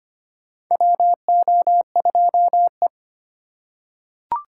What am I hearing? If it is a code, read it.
Morse code, heard as a keyed tone, about 25 wpm, 700 Hz. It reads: WO2E
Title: K1USN SST QSO Practice - 25wpm